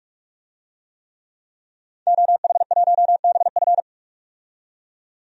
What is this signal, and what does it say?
Morse code, heard as a keyed tone, about 46 wpm, 700 Hz. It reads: OH1BF
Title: DXCC-100 Call Signs - 5 Characters - 45wpm